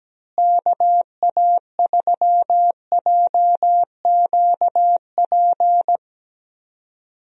Morse code, heard as a keyed tone, about 17 wpm, 700 Hz. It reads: KA3JQP
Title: USA Call Signs - 6 Characters - Repeated 3x - 17wpm